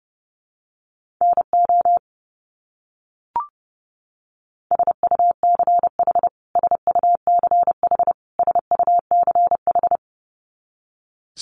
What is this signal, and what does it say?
Morse code, heard as a keyed tone, about 30 wpm, 700 Hz. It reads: NO SUCH SUCH SUCH
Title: Top 200 Words Repeated 30wpm